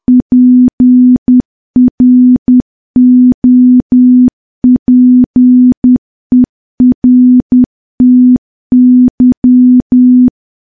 Morse code, heard as a keyed tone, about 10 wpm, 250 Hz. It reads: PROPERTY
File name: morse / morse-training-10WPM